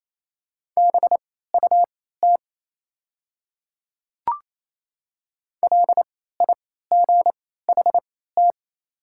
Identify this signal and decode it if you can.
Morse code, heard as a keyed tone, about 27 wpm, spaced as for 16 wpm, 700 Hz. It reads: BUT LIGHT
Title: 3x Character Spacing - Top 200 Words - 28wpm